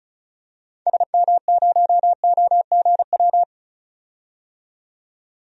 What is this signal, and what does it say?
Morse code, heard as a keyed tone, about 36 wpm, 700 Hz. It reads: SM0OGW
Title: DXCC-100 Call Signs - 6 Characters - 35wpm